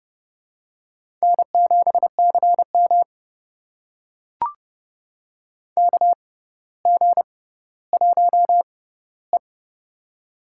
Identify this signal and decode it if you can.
Morse code, heard as a keyed tone, about 30 wpm, spaced as for 15 wpm, 700 Hz. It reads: N7CM KG1E
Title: USA Call Signs - 6x Character Spacing - 30wpm